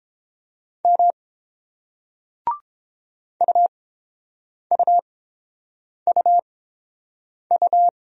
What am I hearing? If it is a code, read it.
Morse code, heard as a keyed tone, about 31 wpm, spaced as for 7 wpm, 700 Hz. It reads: M UUUU